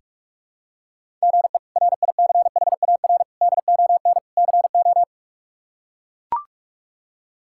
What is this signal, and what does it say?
Morse code, heard as a keyed tone, about 42 wpm, 700 Hz. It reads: GE RICHARD ON CO
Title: K1USN SST QSO Practice - 45wpm